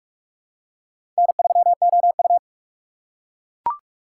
Morse code, heard as a keyed tone, about 42 wpm, 700 Hz. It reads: N3OU
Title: USA Call Signs - 4 Characters - 2x Character Spacing - 45wpm